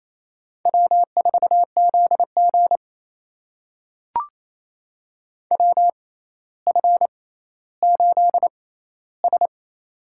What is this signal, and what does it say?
Morse code, heard as a keyed tone, about 28 wpm, spaced as for 14 wpm, 700 Hz. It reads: W4ZG WF8S
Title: USA Call Signs - 6x Character Spacing - 28wpm